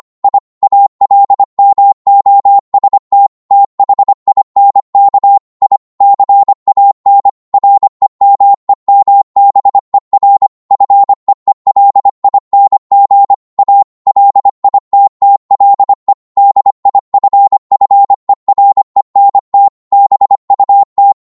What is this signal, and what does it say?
Morse code, heard as a keyed tone, about 25 wpm, 800 Hz. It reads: I ALMOST THINK I CAN REMEMBER FEELING A LITTLE DIFFERENT BUT